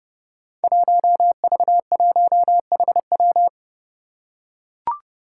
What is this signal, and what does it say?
Morse code, heard as a keyed tone, about 29 wpm, 700 Hz. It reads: 1V1HW
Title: Five Challenging Letter-Number Combinations 30wpm